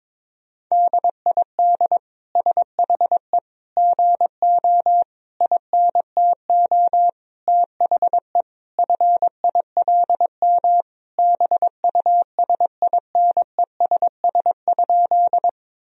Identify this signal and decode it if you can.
Morse code, heard as a keyed tone, about 22 wpm, 700 Hz. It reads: DID SHE GO INTO THE FILM BUSINESS?
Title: Sentences from Top 1000 Words - Review Part 3 - 22wpm